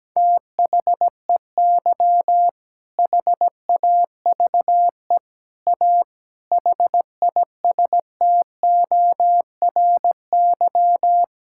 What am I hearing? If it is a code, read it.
Morse code, heard as a keyed tone, about 17 wpm, 700 Hz. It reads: THEY HAVE A HISTORY
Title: Sentences from Top 1000 Words - Review Part 6 - 17wpm